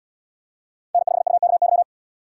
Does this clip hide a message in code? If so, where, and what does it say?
Yes: Morse code from 0.9 to 1.8 s, heard as a keyed tone, about 70 wpm, 700 Hz. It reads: N5UKY